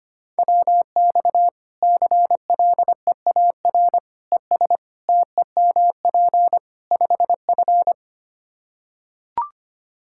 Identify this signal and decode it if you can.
Morse code, heard as a keyed tone, about 25 wpm, 700 Hz. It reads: WX CLEAR ES TEMP 5F
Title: Practice QSO - Exchange 2 - Info - 25wpm